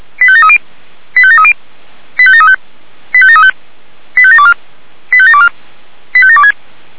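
Five-tone selective call (ZVEI), heard as quick runs of five tones, tones of about 70 ms. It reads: 85730 85730 85735 85735 85724 85724 85726
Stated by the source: ZVEI five-tone selective call 85722 received on an SDR.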